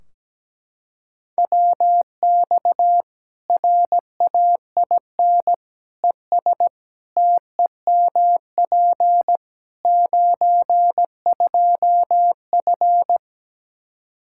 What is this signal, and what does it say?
Morse code, heard as a keyed tone, about 17 wpm, 700 Hz. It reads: WX RAIN ES TEMP 92F
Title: Practice QSO - Exchange 2 - Info - 17wpm